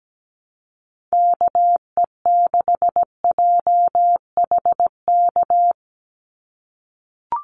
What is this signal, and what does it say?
Morse code, heard as a keyed tone, about 17 wpm, 700 Hz. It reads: KE6JHK
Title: USA Call Signs 17wpm